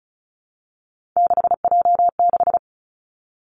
Morse code, heard as a keyed tone, about 36 wpm, 700 Hz. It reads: 6J6
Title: Three Challenging Letter-Number Combinations 35wpm